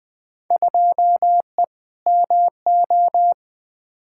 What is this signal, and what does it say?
Morse code, heard as a keyed tone, about 21 wpm, 700 Hz. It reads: 2E MO
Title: Field Day Exchanges - 20wpm